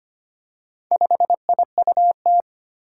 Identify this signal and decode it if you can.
Morse code, heard as a keyed tone, about 25 wpm, 700 Hz. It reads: 5IUT
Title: Four Challenging Letter-Number Combinations 25wpm